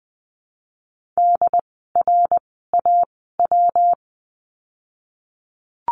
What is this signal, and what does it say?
Morse code, heard as a keyed tone, about 20 wpm, spaced as for 15 wpm, 700 Hz. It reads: DRAW
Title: 115 Lesson 19f - Letter w - Words 20wpm